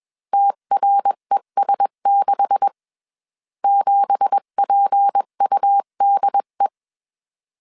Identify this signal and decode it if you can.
Morse code, heard as a keyed tone, about 21 wpm, 800 Hz. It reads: TRES6 7PUDE
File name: CW / DET / 1 9 1500C S B 20wpm 2313